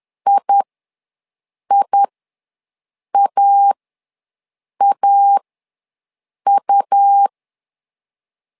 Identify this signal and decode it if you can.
Morse code, heard as a keyed tone, about 11 wpm, spaced as for 6 wpm, 800 Hz. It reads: IIAAU
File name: LEZIONI / 2 UHDN / 2 U ws 10wpm 2017